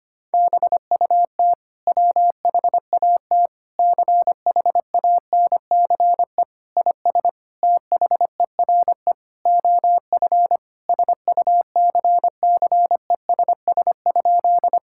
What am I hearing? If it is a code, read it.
Morse code, heard as a keyed tone, about 25 wpm, 700 Hz. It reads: BUT WHAT CHANCE IS THERE OF SUCCESS?
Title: Sentences from Top 1000 Words - Review Part 6 - 25wpm